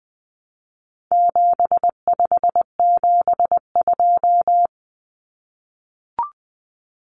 Morse code, heard as a keyed tone, about 20 wpm, 700 Hz. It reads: 7572
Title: Numbers - 4-Digit 20wpm